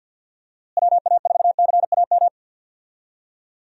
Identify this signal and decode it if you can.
Morse code, heard as a keyed tone, about 51 wpm, 700 Hz. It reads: WA4CAM
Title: USA Call Signs - 6 Characters - Repeated 2x - 50wpm